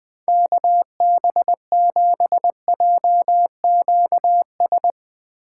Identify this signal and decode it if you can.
Morse code, heard as a keyed tone, about 21 wpm, 700 Hz. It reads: KB7JQS